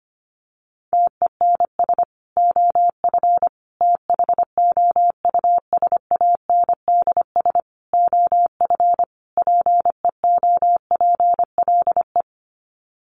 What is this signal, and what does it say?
Morse code, heard as a keyed tone, about 25 wpm, 700 Hz. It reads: TENS OF THOUSANDS OF PEOPLE
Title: Sets of 5 Words Encore2 25wpm